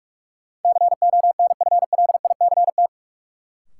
Morse code, heard as a keyed tone, about 42 wpm, 700 Hz. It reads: CONFLICT
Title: Sentences from Top 1300 Words - New Word Drill - 45wpm